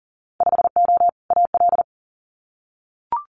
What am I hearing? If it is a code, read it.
Morse code, heard as a keyed tone, about 42 wpm, 700 Hz. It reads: 5O AL
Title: Winter Field Day Exchanges 40wpm